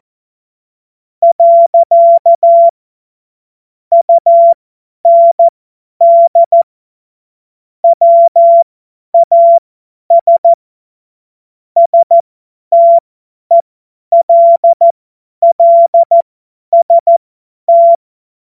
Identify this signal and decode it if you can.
Morse code, heard as a keyed tone, about 14 wpm, spaced as for 10 wpm, 680 Hz. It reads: . UND WAS STELLST
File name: Morsekurs / Witze / Witz 26. Oktober 2021, 50 F70